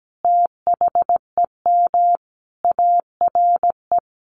Morse code, heard as a keyed tone, about 17 wpm, 700 Hz. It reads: THEM ARE
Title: Sets of 5 Words Encore2 17wpm